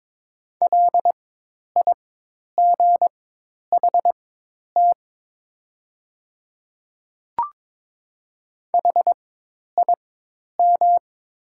Morse code, heard as a keyed tone, about 22 wpm, spaced as for 10 wpm, 700 Hz. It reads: LIGHT HIM
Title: 4x Character Spacing - Top 200 Words - 22wpm